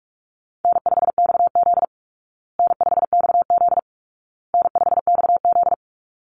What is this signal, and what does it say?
Morse code, heard as a keyed tone, about 44 wpm, 700 Hz. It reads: N5XZ N5XZ N5XZ
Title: Easy Call Signs Repeated 3x - 45wpm